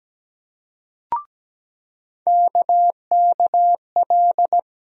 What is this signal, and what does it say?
Morse code, heard as a keyed tone, about 17 wpm, 700 Hz. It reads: KKL
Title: USA Call Sign Postfixes - 3 Letters 17wpm